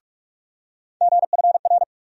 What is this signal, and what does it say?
Morse code, heard as a keyed tone, about 48 wpm, 700 Hz. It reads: GFR
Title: USA Call Sign Postfixes - 3 Letters 45wpm